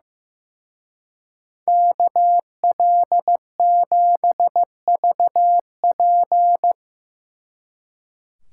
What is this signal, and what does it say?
Morse code, heard as a keyed tone, about 16 wpm, 700 Hz. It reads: KL7VP
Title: Speed Racing - USA 2x2 Call Signs 15wpm